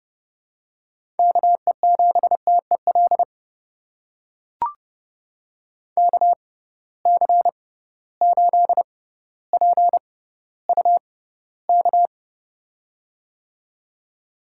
Morse code, heard as a keyed tone, about 32 wpm, spaced as for 16 wpm, 700 Hz. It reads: KE7TEL KC8PUK
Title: USA Call Signs - 6x Character Spacing - 30wpm